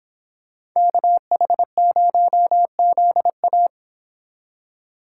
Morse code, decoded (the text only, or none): KH0ZA